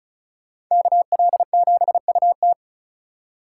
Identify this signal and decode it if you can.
Morse code, heard as a keyed tone, about 46 wpm, 700 Hz. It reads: KL7UT